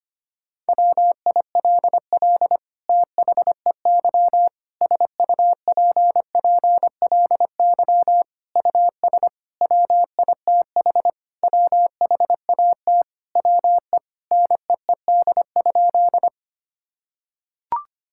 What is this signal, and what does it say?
Morse code, heard as a keyed tone, about 25 wpm, 700 Hz. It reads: WILL THEY SUPPLY US WITH WHAT WE NEED?
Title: Sentences from Top 1000 Words - Review Part 5 - 25wpm